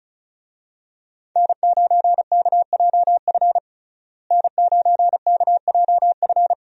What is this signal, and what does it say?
Morse code, heard as a keyed tone, about 36 wpm, 700 Hz. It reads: N9KJF N9KJF
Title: USA Call Signs - 5 Characters - Repeated 3x - 35wpm